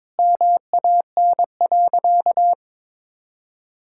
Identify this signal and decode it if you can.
Morse code, heard as a keyed tone, about 22 wpm, 700 Hz. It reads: MAN.
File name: practice / 22wpm / Sentences with 3-Word Phrases - 22wpm